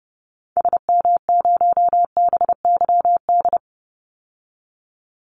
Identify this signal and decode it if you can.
Morse code, heard as a keyed tone, about 30 wpm, 700 Hz. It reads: SM0BYD
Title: Call Signs Repeated 2x - 30wpm